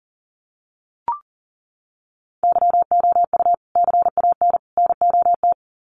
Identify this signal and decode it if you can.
Morse code, heard as a keyed tone, about 40 wpm, 700 Hz. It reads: YOU CAN NOT
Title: Sets of 3 Words 40wpm